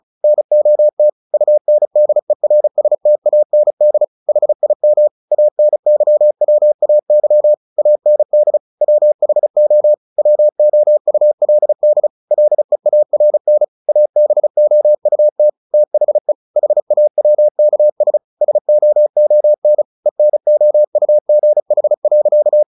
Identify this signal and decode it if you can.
Morse code, heard as a keyed tone, about 34 wpm, 600 Hz. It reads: NOT UNDERSTAND HIM ANYWAY AND WHO WOULD LEARN ABOUT THE HAWKS SOON ENOUGH.
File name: the old man and the sea / the old man and the sea 1 35wpm